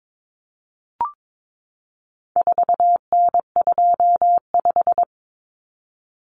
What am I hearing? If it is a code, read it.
Morse code, heard as a keyed tone, about 22 wpm, 700 Hz. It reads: 4N25
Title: Common Cut Numbers - 4-Digit 22wpm